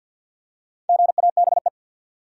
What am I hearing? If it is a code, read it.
Morse code, heard as a keyed tone, about 50 wpm, 700 Hz. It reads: GABE
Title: USA Common Names - Max 5 Letters - 50wpm